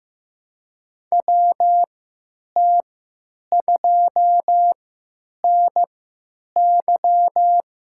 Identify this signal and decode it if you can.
Morse code, heard as a keyed tone, about 15 wpm, spaced as for 8 wpm, 700 Hz. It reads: WT2NY
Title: USA Call Signs - 3x Character Spacing - 15wpm